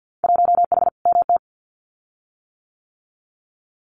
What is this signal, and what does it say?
Morse code, heard as a keyed tone, about 50 wpm, 700 Hz. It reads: 1H MT
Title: Winter Field Day Exchanges Encore 50wpm